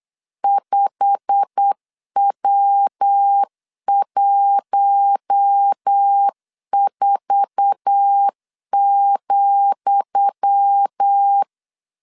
Morse code, heard as a keyed tone, about 8 wpm, 800 Hz. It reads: 5W14,
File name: LEZIONI / 10 SIGNS / 10D COMA 8wpm 3749